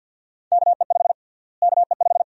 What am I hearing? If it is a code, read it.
Morse code, heard as a keyed tone, about 52 wpm, 700 Hz. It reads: KE5 KE5
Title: USA Call Sign Prefixes - Repeated 2x - 50wpm